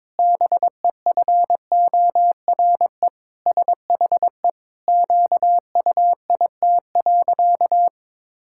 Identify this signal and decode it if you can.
Morse code, heard as a keyed tone, about 21 wpm, 700 Hz. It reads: BEFORE SHE QUIT.
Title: Pangram Challenge v2 - 22wpm